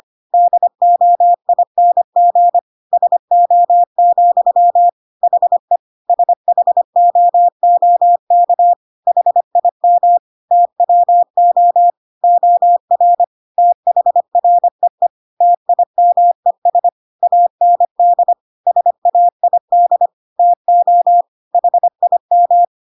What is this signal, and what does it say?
Morse code, heard as a keyed tone, about 25 wpm, 700 Hz. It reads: DOING SO, HE SHOOK HIM TWO OR THREE TIMES AND SAID TO HIM
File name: Pinocchio / Pinocchio ch3 25 25 ik1hge